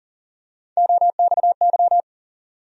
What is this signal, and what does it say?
Morse code, heard as a keyed tone, about 40 wpm, 700 Hz. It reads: OXY
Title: USA Call Sign Postfixes - 3 Letters 40wpm